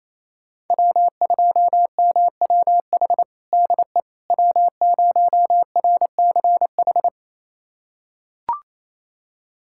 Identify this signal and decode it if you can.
Morse code, heard as a keyed tone, about 28 wpm, 700 Hz. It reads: W2MWH DE W0RCH
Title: USA Callsign DE Callsign - 5 Characters - 28wpm